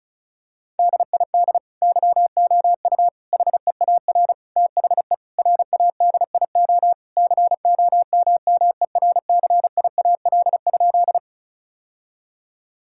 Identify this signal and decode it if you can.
Morse code, heard as a keyed tone, about 34 wpm, 700 Hz. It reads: DID YOU HEAR THE RADIO COMMERCIAL?
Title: Sentences from Top 1000 Words - Review Part 5 - 35wpm